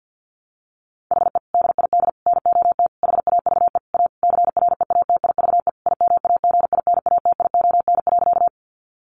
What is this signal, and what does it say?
Morse code, heard as a keyed tone, about 50 wpm, 700 Hz. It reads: HE DID NOT HAVE A CREATIVE IMAGINATION.
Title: Sentences from Top 1300 Words 50wpm